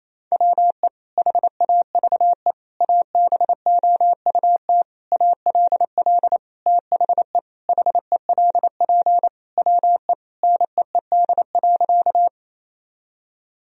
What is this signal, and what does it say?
Morse code, heard as a keyed tone, about 28 wpm, 700 Hz. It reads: WE HAVE ABOUT ALL THE HELP WE NEED.
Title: Sentences from Top 1000 Words - Review Part 1 - 28wpm